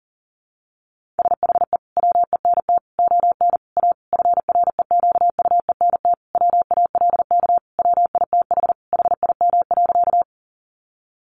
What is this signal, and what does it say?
Morse code, heard as a keyed tone, about 40 wpm, 700 Hz. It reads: SHE WENT ON A FREQUENT WALK WITH HIM.
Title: Sentences from Top 1400 Words 40wpm